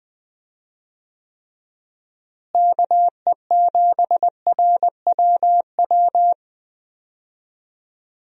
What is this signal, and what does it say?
Morse code, heard as a keyed tone, about 21 wpm, 700 Hz. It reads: KE7RWW
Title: USA Call Signs - 6 Characters - Repeated 2x - 20wpm